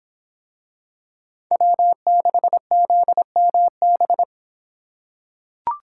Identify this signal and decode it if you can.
Morse code, heard as a keyed tone, about 24 wpm, 700 Hz. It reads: W6ZMB